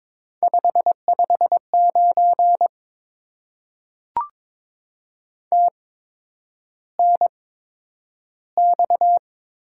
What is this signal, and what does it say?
Morse code, heard as a keyed tone, about 22 wpm, spaced as for 9 wpm, 700 Hz. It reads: 559 TNX
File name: Top 90 QSO Elements - 8x Character Spacing - 22wpm